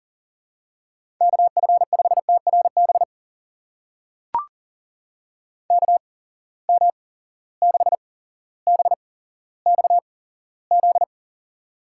Morse code, heard as a keyed tone, about 42 wpm, spaced as for 17 wpm, 700 Hz. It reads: KF5TRB KM6BXZ
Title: USA Call Signs - 6 Characters - 8x Character Spacing - 40wpm